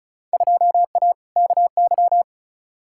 Morse code, heard as a keyed tone, about 36 wpm, 700 Hz. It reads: 2A KY
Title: Field Day Exchanges - 35wpm